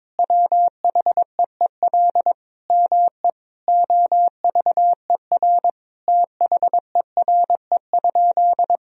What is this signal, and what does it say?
Morse code, heard as a keyed tone, about 22 wpm, 700 Hz. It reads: WHEEL ME OVER THERE?
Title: Sentences from Top 1000 Words - Review Part 3 - 22wpm